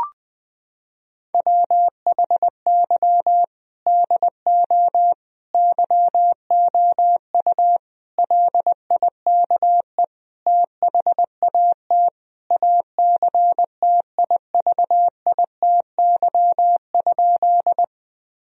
Morse code, heard as a keyed tone, about 20 wpm, 700 Hz. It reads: WHY DO YOU LIKE THAT ACTIVITY?
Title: Sentences from Top 1000 Words - Review Part 4 - 20wpm